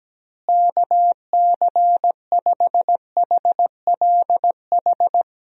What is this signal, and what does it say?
Morse code, heard as a keyed tone, about 17 wpm, 700 Hz. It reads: KC5HLH
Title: USA Call Signs - 6 Characters - Repeated 2x - 17wpm